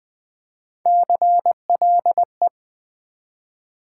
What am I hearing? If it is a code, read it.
Morse code, heard as a keyed tone, about 20 wpm, 700 Hz. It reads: CLE
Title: Build A Word - 1 Letter At A Time - 3 to 5-Letter Words - 20wpm